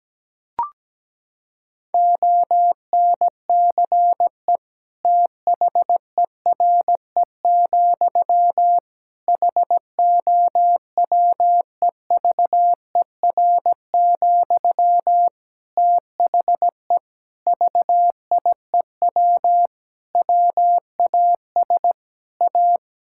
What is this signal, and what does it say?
Morse code, heard as a keyed tone, about 17 wpm, 700 Hz. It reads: ONCE THERE, HOWEVER, THE VIEW WAS A
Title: Sentences from Top 1000 Words - Review Part 6 - 17wpm